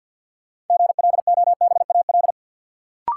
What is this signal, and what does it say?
Morse code, heard as a keyed tone, about 52 wpm, 700 Hz. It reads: GLOBAL
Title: Sentences from Top 1200 Words - New Word Drill - 50wpm